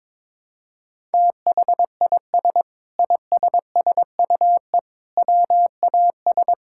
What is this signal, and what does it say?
Morse code, heard as a keyed tone, about 22 wpm, 700 Hz. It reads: THIS ISSUE WAS
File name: Sentences with 7-Word Phrases - 22wpm